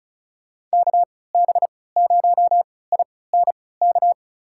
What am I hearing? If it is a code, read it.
Morse code, heard as a keyed tone, about 36 wpm, spaced as for 20 wpm, 700 Hz. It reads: KB0INK
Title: USA Call Signs - 6 Characters - 3x Character Spacing - 35wpm